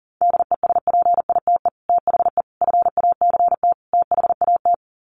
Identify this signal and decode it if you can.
Morse code, heard as a keyed tone, about 37 wpm, 700 Hz. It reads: DESPITE THE FACT THAT
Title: Sets of 4 Words 40wpm